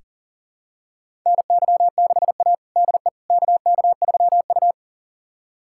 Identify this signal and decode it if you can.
Morse code, heard as a keyed tone, about 40 wpm, 700 Hz. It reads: NY6A DE KK3U
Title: USA Callsign DE Callsign - 4 Characters - 40wpm